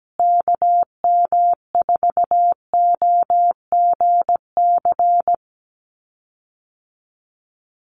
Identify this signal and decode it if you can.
Morse code, heard as a keyed tone, about 17 wpm, 700 Hz. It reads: KM4OGC
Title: USA Call Signs Repeated 3x 17wpm